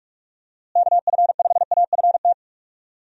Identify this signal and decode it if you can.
Morse code, heard as a keyed tone, about 43 wpm, 700 Hz. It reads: KF5AFT
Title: USA Call Signs - 6 Characters - Repeated 3x - 45wpm